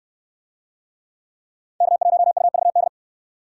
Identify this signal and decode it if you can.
Morse code, heard as a keyed tone, about 67 wpm, 700 Hz. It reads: K1UVD